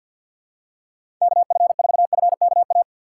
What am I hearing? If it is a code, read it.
Morse code, heard as a keyed tone, about 51 wpm, 700 Hz. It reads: KF4FKA
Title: USA Call Signs - 6 Characters - 6x Character Spacing - 50wpm